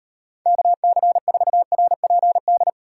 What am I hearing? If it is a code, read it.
Morse code, heard as a keyed tone, about 38 wpm, 700 Hz. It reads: KC4RPD